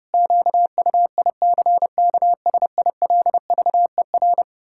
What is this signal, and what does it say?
Morse code, heard as a keyed tone, about 30 wpm, 700 Hz. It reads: QUICKSILVER
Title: Pangram Challenge v2 - 30wpm